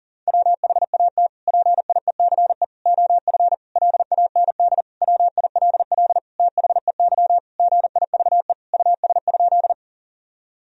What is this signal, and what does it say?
Morse code, heard as a keyed tone, about 40 wpm, 700 Hz. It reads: WHAT PIECE OF LAND WILL THEY GIVE US?
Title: Sentences from Top 1000 Words - Review Part 2 - 40wpm